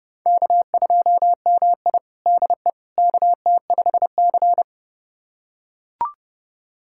Morse code, heard as a keyed tone, about 32 wpm, 700 Hz. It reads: K2MI DE KT5C
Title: USA Callsign DE Callsign - 4 Characters - 30wpm